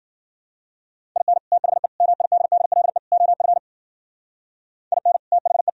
Morse code, heard as a keyed tone, about 52 wpm, 700 Hz. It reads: IN THE MIDDLE OF IN THE